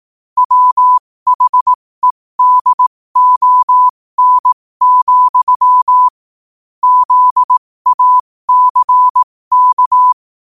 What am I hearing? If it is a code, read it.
Morse code, heard as a keyed tone, about 18 wpm, spaced as for 15 wpm, 1000 Hz. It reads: WHEDON, ZACK
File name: qotdfiles / 15wpm